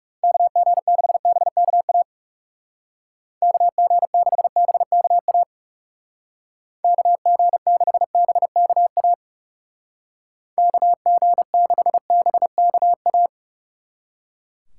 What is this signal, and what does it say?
Morse code, heard as a keyed tone, about 36 wpm, 700 Hz. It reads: KG6BKA KG6BKA KG6BKA KG6BKA
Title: Speed Racing - USA 2x3 Call Signs 30wpm